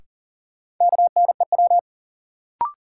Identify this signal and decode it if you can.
Morse code, heard as a keyed tone, about 40 wpm, 700 Hz. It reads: KNEW
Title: Sentences from Top 400 Words - New Word Drill - 40wpm